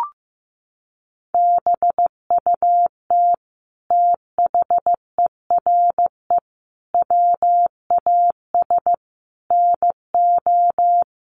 Morse code, heard as a keyed tone, about 15 wpm, 700 Hz. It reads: BUT THERE WAS NO
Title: Sentences from Top 1400 Words 15wpm